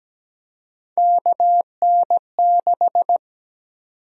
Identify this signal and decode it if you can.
Morse code, heard as a keyed tone, about 17 wpm, 700 Hz. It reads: KN6
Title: USA Call Sign Prefixes - Repeated 2x - 17wpm